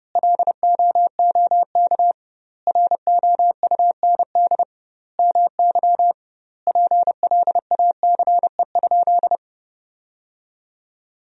Morse code, heard as a keyed tone, about 32 wpm, 700 Hz. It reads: LOOK ROUND MY PLACE?
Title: 2x Word-Spacing - Sentences from Top 200 Words - 30wpm